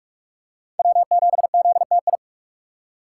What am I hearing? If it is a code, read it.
Morse code, heard as a keyed tone, about 42 wpm, 700 Hz. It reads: W7ZTI